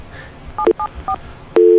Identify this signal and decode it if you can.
DTMF, heard as telephone touch tones, tones of about 90 ms. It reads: 774